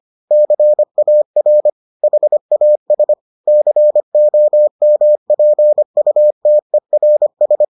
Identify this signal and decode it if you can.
Morse code, heard as a keyed tone, about 25 wpm, 600 Hz. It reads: CAR HAS COMPUTERS